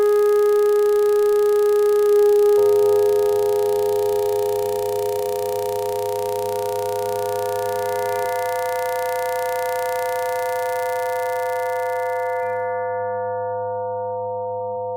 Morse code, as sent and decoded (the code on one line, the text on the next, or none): none